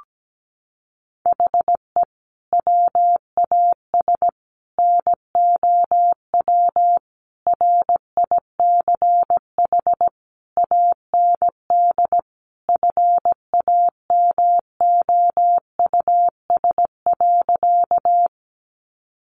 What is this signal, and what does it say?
Morse code, heard as a keyed tone, about 17 wpm, 700 Hz. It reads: HE WAS NOW RICH AND FAMOUS.
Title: Sentences from Top 1500 Words 17wpm